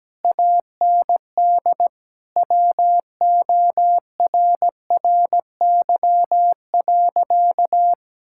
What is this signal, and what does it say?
Morse code, heard as a keyed tone, about 17 wpm, 700 Hz. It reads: AND WORRY.
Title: Pangram Challenge v2 - 17wpm